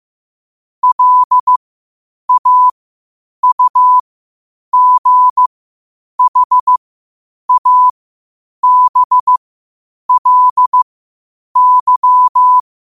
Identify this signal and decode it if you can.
Morse code, heard as a keyed tone, about 15 wpm, spaced as for 8 wpm, 1000 Hz. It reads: LAUGHABLY
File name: qotdfiles / 8wpm